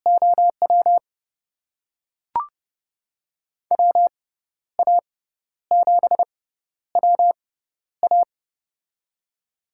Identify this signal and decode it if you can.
Morse code, heard as a keyed tone, about 30 wpm, spaced as for 12 wpm, 700 Hz. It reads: OW WA7WA
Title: USA Call Signs - 5 Characters - 6x Character Spacing - 30wpm